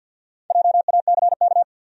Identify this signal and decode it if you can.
Morse code, heard as a keyed tone, about 53 wpm, 700 Hz. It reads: JACK
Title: USA Common Names - Max 4 Letters - 50wpm